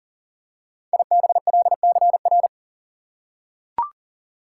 Morse code, heard as a keyed tone, about 40 wpm, 700 Hz. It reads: IBLCR
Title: Code Groups - 5 Letter:Numbers - 40wpm